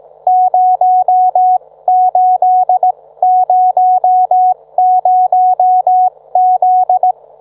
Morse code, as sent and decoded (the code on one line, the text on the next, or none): ----- ---.. ----- ----- --..
0800Z